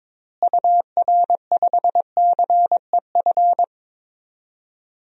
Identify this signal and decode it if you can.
Morse code, heard as a keyed tone, about 22 wpm, 700 Hz. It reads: UR5CEF
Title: DXCC-100 Call Signs - 6 Characters - Repeated 3x - 22wpm